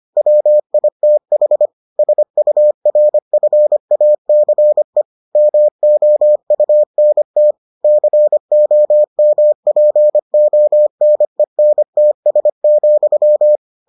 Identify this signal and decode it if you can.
Morse code, heard as a keyed tone, about 25 wpm, 600 Hz. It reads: WITH SURFACE MOUNT COMPONENTS,